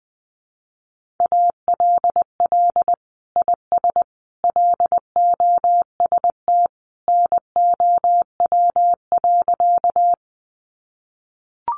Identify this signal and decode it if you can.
Morse code, heard as a keyed tone, about 20 wpm, 700 Hz. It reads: ALL IS LOST NOW.
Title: Sentences from Top 1400 Words 20wpm